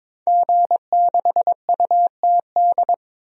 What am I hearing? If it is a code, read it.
Morse code, heard as a keyed tone, about 22 wpm, 700 Hz. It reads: G6UTD
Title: DXCC-100 Call Signs - 5 Characters - 22wpm